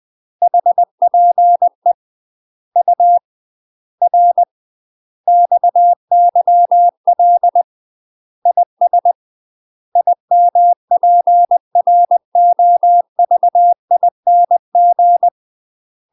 Morse code, heard as a keyed tone, about 20 wpm, 700 Hz. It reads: HPE U R XYL IS IMPROVING
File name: UK MASTER QSOs TRAINING / 20WPM / UK MASTER QSOs 0022